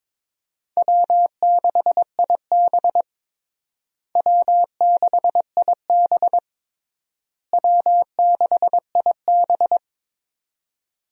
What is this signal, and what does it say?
Morse code, heard as a keyed tone, about 22 wpm, 700 Hz. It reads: W6IB W6IB W6IB
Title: USA Call Signs - 4 Characters - Repeated 3x - 22wpm